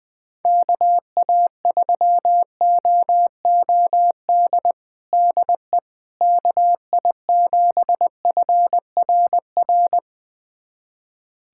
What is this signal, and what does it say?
Morse code, heard as a keyed tone, about 21 wpm, 690 Hz. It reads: KA3OOD DE KI7FRR
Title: USA Callsign DE Callsign 20wpm